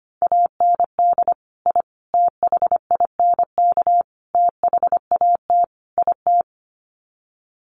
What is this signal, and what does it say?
Morse code, heard as a keyed tone, about 25 wpm, 700 Hz. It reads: AND I THINK THAT IT
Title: Sets of 5 Words Encore2 25wpm